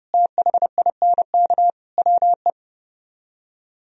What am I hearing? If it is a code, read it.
Morse code, heard as a keyed tone, about 37 wpm, 700 Hz. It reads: THINK WE
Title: Speed-Racing - Sets of 4 Words 30wpm